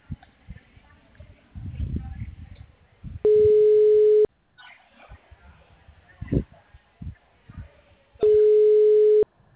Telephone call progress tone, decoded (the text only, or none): ringback tone